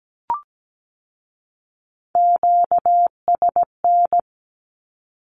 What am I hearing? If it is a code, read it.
Morse code, heard as a keyed tone, about 17 wpm, 700 Hz. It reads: QSN
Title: Q-Codes with Spoken Meaning - 17wpm